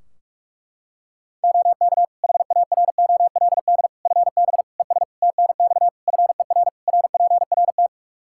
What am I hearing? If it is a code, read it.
Morse code, heard as a keyed tone, about 41 wpm, 700 Hz. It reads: OK HAROLD FB ES TNX FER RPRT
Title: Practice QSO - Exchange 2 - Info - 45wpm